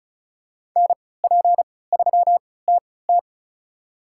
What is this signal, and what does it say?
Morse code, heard as a keyed tone, about 34 wpm, spaced as for 20 wpm, 700 Hz. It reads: NP3TT
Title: USA Call Signs - 5 Characters - 3x Character Spacing - 35wpm